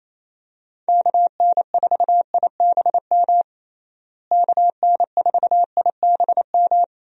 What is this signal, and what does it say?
Morse code, heard as a keyed tone, about 28 wpm, 700 Hz. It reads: KN4IBM KN4IBM
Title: USA Call Signs - 6 Characters - Repeated 2x - 28wpm